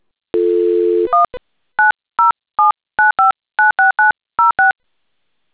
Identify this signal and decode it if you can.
DTMF, heard as telephone touch tones, tones of about 150 ms. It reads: 19079596906